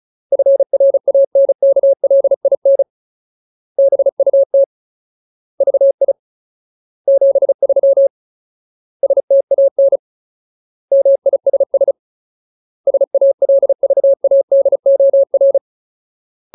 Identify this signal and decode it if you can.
Morse code, heard as a keyed tone, about 34 wpm, 550 Hz. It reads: FRANKLIN BUT VI 73 STAN MISS SALVADOR
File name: Ultimate Word List / 35WPM Ultimate